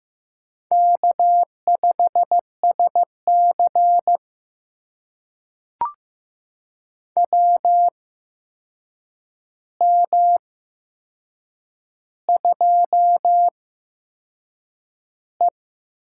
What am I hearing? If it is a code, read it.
Morse code, heard as a keyed tone, about 15 wpm, spaced as for 6 wpm, 700 Hz. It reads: K5SC WM2E
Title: USA Call Signs - 4 Characters - 8x Character Spacing - 15wpm